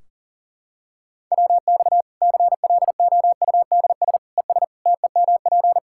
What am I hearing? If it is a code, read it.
Morse code, heard as a keyed tone, about 40 wpm, 700 Hz. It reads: WX CLOUDS ES TEMP
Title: Practice QSO - Exchange 2 - Info - 40wpm